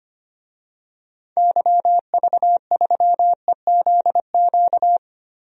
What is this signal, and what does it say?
Morse code, heard as a keyed tone, about 25 wpm, 700 Hz. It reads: YV3EZQ
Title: DXCC-100 Call Signs - 6 Characters - 25wpm